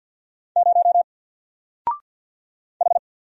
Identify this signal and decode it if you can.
Morse code, heard as a keyed tone, about 50 wpm, 700 Hz. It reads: 0 H
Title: Single Letter-Number Alt 50wpm